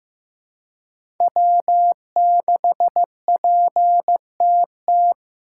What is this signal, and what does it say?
Morse code, heard as a keyed tone, about 15 wpm, 700 Hz. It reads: W6PTT
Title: USA Call Signs - 5 Characters - 3x Character Spacing - 15wpm